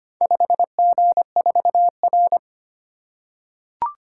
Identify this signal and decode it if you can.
Morse code, heard as a keyed tone, about 25 wpm, 700 Hz. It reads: 5G4R
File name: practice / 25wpm / Four Letter-Number Combinations - 25wpm